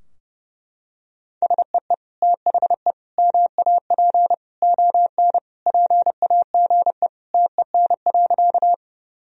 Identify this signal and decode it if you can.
Morse code, heard as a keyed tone, about 30 wpm, 700 Hz. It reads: SEE THE MAP ON PAGE TEN.
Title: Sentences from Top 1000 Words - Review Part 2 - 30wpm